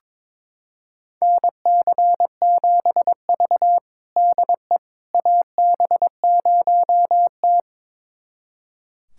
Morse code, heard as a keyed tone, about 22 wpm, 700 Hz. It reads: NC7V DE AB0T
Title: USA Callsign DE Callsign - 4 Characters - 22wpm